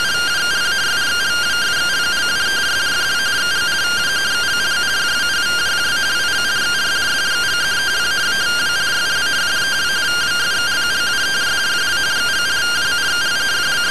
Two-tone teleprinter signal, about 170 Hz shift.